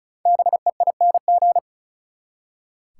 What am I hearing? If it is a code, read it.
Morse code, heard as a keyed tone, about 36 wpm, 700 Hz. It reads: BEING
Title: Five Letter Words - 35wpm